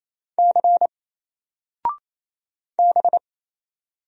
Morse code, heard as a keyed tone, about 28 wpm, 700 Hz. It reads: C B
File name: Single-Letter-Number-28wpm